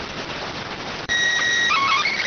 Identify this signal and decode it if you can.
SSTV (slow-scan television), PD 290.